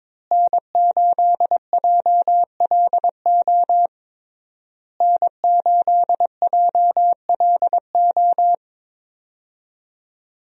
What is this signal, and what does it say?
Morse code, heard as a keyed tone, about 21 wpm, 700 Hz. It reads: N8JLO N8JLO
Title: USA Call Signs - 5 Characters - Repeated 3x - 22wpm